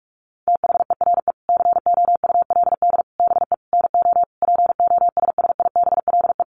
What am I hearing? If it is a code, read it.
Morse code, heard as a keyed tone, about 44 wpm, 700 Hz. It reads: THERE COULD BE NO POSSIBLE
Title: Sentences from Top 700 Words 45wpm